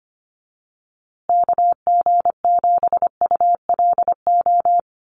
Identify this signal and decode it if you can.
Morse code, heard as a keyed tone, about 25 wpm, 700 Hz. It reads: KG7ULO
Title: USA Call Signs Repeated 3x 25wpm